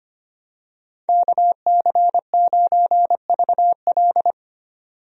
Morse code, heard as a keyed tone, about 25 wpm, 700 Hz. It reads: KC9VL
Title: USA Call Signs - 5 Characters - 25wpm